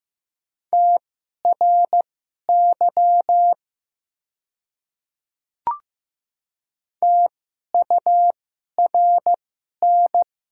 Morse code, heard as a keyed tone, about 15 wpm, spaced as for 11 wpm, 700 Hz. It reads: TRY TURN
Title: 2x Character Spacing - Top 200 Words - 15wpm